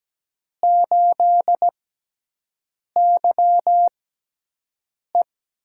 Morse code, heard as a keyed tone, about 17 wpm, spaced as for 6 wpm, 700 Hz. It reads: 8YE
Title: USA Call Signs - 4 Characters - 6x Character Spacing - 17wpm